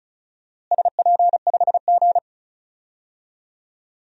Morse code, heard as a keyed tone, about 34 wpm, 700 Hz. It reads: SP5G
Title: DXCC-100 Call Signs - 4 Characters - Repeated 3x - 35wpm